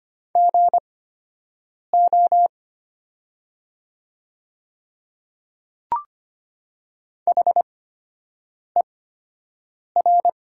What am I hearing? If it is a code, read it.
Morse code, heard as a keyed tone, about 25 wpm, spaced as for 7 wpm, 700 Hz. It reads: GO HER
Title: 8x Character Spacing - Top 200 Words - 25wpm